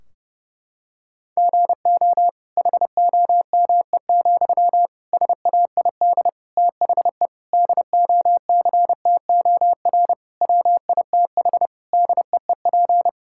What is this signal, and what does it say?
Morse code, heard as a keyed tone, about 30 wpm, 700 Hz. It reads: GO HOME, SAID THE DOCTOR WITH DEEP
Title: Sentences from Top 1600 Words 30wpm